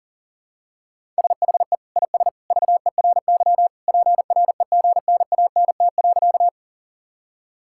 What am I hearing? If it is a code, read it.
Morse code, heard as a keyed tone, about 40 wpm, 700 Hz. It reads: SHE IS VERY PREGNANT.
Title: Sentences from Top 1000 Words - Review Part 6 - 40wpm